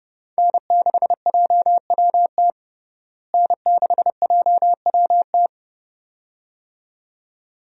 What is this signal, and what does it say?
Morse code, heard as a keyed tone, about 30 wpm, 700 Hz. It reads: N6JWT N6JWT
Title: USA Call Signs - 5 Characters - Repeated 2x - 30wpm